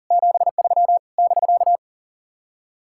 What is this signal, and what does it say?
Morse code, heard as a keyed tone, about 49 wpm, 700 Hz. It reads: 73 <BK>
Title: Speed-Racing Common QSO Phrases - 40wpm